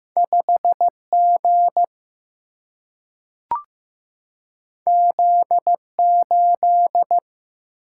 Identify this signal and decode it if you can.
Morse code, heard as a keyed tone, about 15 wpm, 700 Hz. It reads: 5G Z8
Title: Single-Letter Single-Number 15wpm